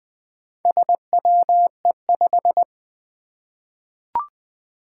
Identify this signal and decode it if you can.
Morse code, heard as a keyed tone, about 20 wpm, 700 Hz. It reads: SWE5
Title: Four Challenging Letter-Number Combinations 20wpm